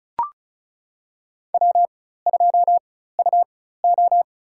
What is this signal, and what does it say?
Morse code, heard as a keyed tone, about 34 wpm, spaced as for 16 wpm, 700 Hz. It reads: W2UO
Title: USA Call Signs - 4 Characters - 4x Character Spacing - 35wpm